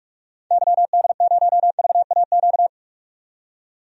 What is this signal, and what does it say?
Morse code, heard as a keyed tone, about 42 wpm, 700 Hz. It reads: YD0VAQ
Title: DXCC-100 Call Signs - 6 Characters - 45wpm